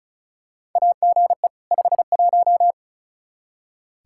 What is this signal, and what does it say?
Morse code, heard as a keyed tone, about 36 wpm, 700 Hz. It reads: AGE 51
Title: Practice QSO - Exchange 3 - More Info - 35wpm